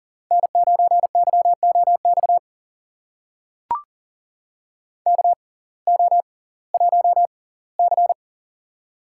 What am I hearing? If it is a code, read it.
Morse code, heard as a keyed tone, about 40 wpm, spaced as for 22 wpm, 700 Hz. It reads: N9YOX KO1C
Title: USA Call Signs - 6x Character Spacing - 40wpm